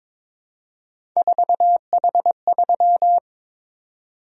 Morse code, heard as a keyed tone, about 22 wpm, 700 Hz. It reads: 4H3
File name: practice / 22wpm / Three Letter-Number Combinations - 22wpm